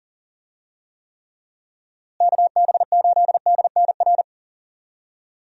Morse code, heard as a keyed tone, about 40 wpm, 700 Hz. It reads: KB8DNR
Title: USA Call Signs - 6 Characters - Repeated 2x - 40wpm